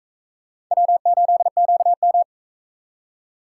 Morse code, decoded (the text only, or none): W8QM